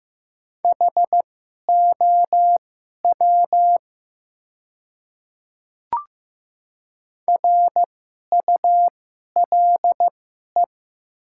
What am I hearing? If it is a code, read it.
Morse code, heard as a keyed tone, about 15 wpm, spaced as for 11 wpm, 700 Hz. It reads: HOW RULE
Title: 2x Character Spacing - Top 500 Words - 15wpm